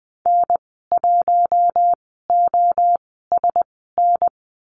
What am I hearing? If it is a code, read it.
Morse code, heard as a keyed tone, about 20 wpm, spaced as for 14 wpm, 700 Hz. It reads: N1OSN
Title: 069 Lesson 12g - Letter d - Call Signs 20wpm